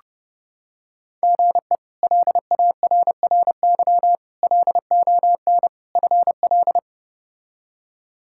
Morse code, heard as a keyed tone, about 30 wpm, 700 Hz. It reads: GE LARRY LON FL
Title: K1USN SST QSO Practice - 30wpm